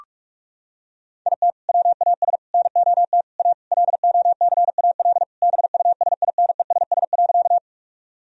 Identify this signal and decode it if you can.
Morse code, heard as a keyed tone, about 46 wpm, 700 Hz. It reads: IT WAS NOT A LOCAL BUSINESS.
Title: Sentences from Top 1000 Words - Review Part 3 - 45wpm